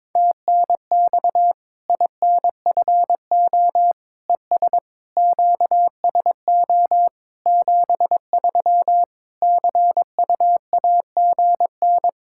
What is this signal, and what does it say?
Morse code, heard as a keyed tone, about 22 wpm, 700 Hz. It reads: TNX INFO ES QSO 73 CUAGN
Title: Practice QSO - Exchange 4 - Ending - 22wpm